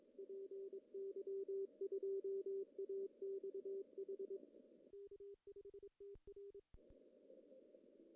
Morse code, heard as a keyed tone, about 23 wpm, 410 Hz. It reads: PY2AXH K5TR
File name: CW / cqww-cw-k5tr